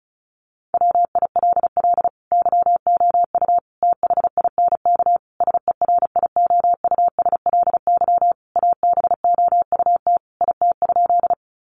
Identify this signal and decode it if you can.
Morse code, heard as a keyed tone, about 34 wpm, 700 Hz. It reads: WILL YOU THINK SERIOUSLY ABOUT IT?
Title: Sentences from Top 1200 Words 35wpm